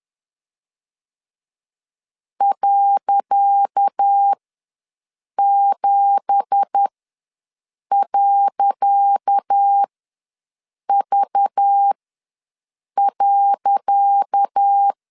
Morse code, heard as a keyed tone, about 10 wpm, spaced as for 6 wpm, 800 Hz. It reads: .7.V.
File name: LEZIONI / 10 SIGNS / 10C DOT ws 10wpm 4213